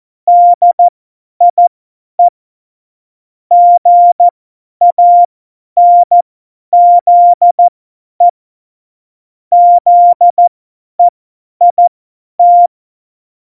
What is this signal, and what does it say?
Morse code, heard as a keyed tone, about 14 wpm, spaced as for 10 wpm, 700 Hz. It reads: DIE GANZE ZEIT
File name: Morsekurs / Witze / Witz 27. August 2019, 50 F70